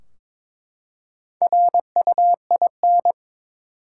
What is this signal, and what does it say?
Morse code, heard as a keyed tone, about 22 wpm, 700 Hz. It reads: RUIN
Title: Four Letter Words - 22wpm